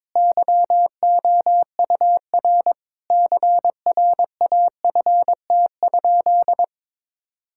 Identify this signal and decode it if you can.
Morse code, heard as a keyed tone, about 22 wpm, 700 Hz. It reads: YOUR CRAFT?
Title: Sentences from Top 1000 Words - Review Part 6 - 22wpm